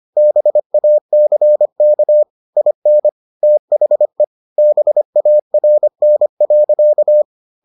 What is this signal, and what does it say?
Morse code, heard as a keyed tone, about 25 wpm, 600 Hz. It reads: BACK IN THE BARN.